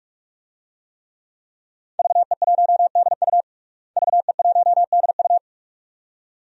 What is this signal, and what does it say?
Morse code, heard as a keyed tone, about 48 wpm, 700 Hz. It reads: VE1DU VE1DU
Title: DXCC-100 Call Signs - 5 Characters - Repeated 2x - 45wpm